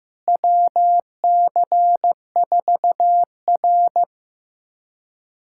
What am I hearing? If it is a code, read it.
Morse code, heard as a keyed tone, about 16 wpm, 700 Hz. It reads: WC4R
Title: USA Callsign DE Callsign - 4 Characters - 15wpm